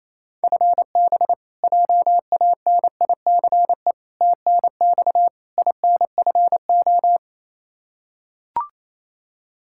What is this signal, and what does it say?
Morse code, heard as a keyed tone, about 27 wpm, 700 Hz. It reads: FB JANICE TNX INFO
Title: K1USN SST QSO Practice 28wpm